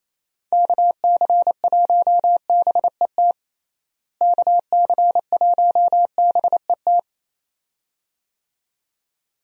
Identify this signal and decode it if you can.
Morse code, heard as a keyed tone, about 28 wpm, 700 Hz. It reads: KC1BET KC1BET
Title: USA Call Signs - 6 Characters - Repeated 2x - 28wpm